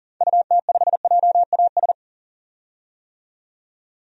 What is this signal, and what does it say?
Morse code, heard as a keyed tone, about 37 wpm, 700 Hz. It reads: UT5JAS
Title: DXCC-100 Call Signs - 6 Characters - 40wpm